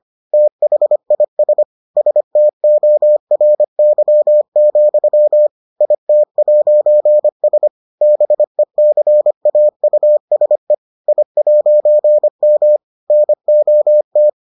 Morse code, heard as a keyed tone, about 25 wpm, 600 Hz. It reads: THIS STORY, IT'S BECAUSE I'M NOT